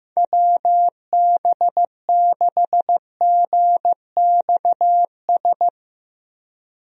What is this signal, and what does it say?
Morse code, heard as a keyed tone, about 15 wpm, 700 Hz. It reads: WB6GXS
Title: USA Call Signs - 6 Characters - 15wpm